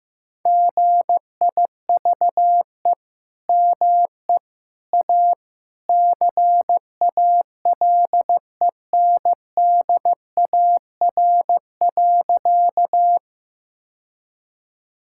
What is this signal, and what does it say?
Morse code, heard as a keyed tone, about 16 wpm, 700 Hz. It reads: GIVE ME A CALENDAR.
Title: Sentences from Top 1600 Words 15wpm